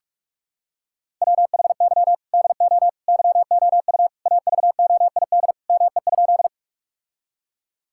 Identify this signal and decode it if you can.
Morse code, heard as a keyed tone, about 47 wpm, 700 Hz. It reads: WHY DO YOU AVOID ME?
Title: Sentences from Top 1000 Words - Review Part 6 - 45wpm